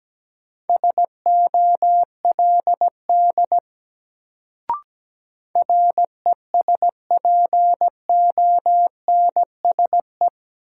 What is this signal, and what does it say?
Morse code, heard as a keyed tone, about 17 wpm, 700 Hz. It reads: SOLD RESPONSE
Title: Sentences from Top 1400 Words - New Word Drill Alt - 17wpm